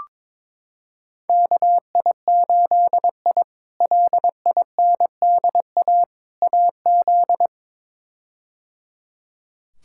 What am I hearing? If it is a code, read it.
Morse code, heard as a keyed tone, about 22 wpm, 700 Hz. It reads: KI8I LINDA AZ
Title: K1USN SST QSO Practice - 22wpm